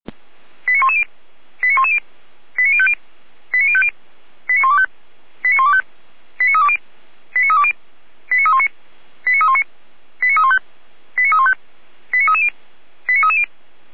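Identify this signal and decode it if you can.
Five-tone selective call (ZVEI), heard as quick runs of five tones, tones of about 70 ms. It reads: 89110 89110 89050 89050 89125 89125 89320 89320 89319 89319 89315 89315 89330 89330
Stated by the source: Received by SDR, ZVEI five-tone selective call.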